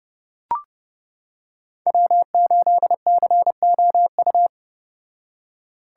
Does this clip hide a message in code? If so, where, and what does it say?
Yes: Morse code from 1.9 to 4.5 s, heard as a keyed tone, about 30 wpm, 700 Hz. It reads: W8COU